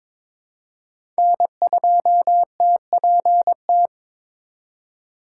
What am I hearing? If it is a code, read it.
Morse code, heard as a keyed tone, about 22 wpm, 700 Hz. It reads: N2TPT
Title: USA Call Signs - 5 Characters - 2x Character Spacing - 22wpm